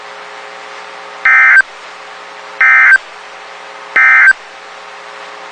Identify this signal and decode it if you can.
Emergency Alert System (SAME) tones, heard as harsh buzzing bursts. It is heard from 1.1 to 4.4 s.
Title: WXK91 NOAA Weather Radio